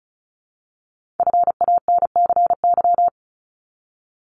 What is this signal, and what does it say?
Morse code, heard as a keyed tone, about 36 wpm, 700 Hz. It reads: FANCY
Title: Multi-letter Phonemes Repeated 3x 35wpm